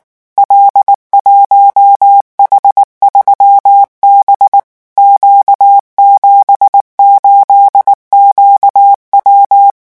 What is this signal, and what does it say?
Morse code, heard as a keyed tone, about 19 wpm, 800 Hz. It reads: L1H3B Q78QW
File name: morse mp3 / revision12-vitesse 18 mots